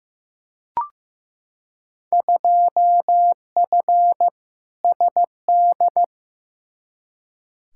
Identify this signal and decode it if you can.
Morse code, heard as a keyed tone, about 15 wpm, 700 Hz. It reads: 2F SD
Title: Field Day Exchanges Encore - 15wpm